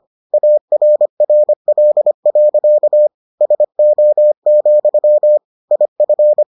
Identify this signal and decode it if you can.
Morse code, heard as a keyed tone, about 25 wpm, 600 Hz. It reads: ARRL. SO, IF